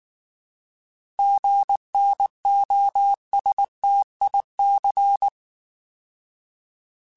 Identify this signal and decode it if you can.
Morse code, heard as a keyed tone, about 19 wpm, 800 Hz. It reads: GNOSTIC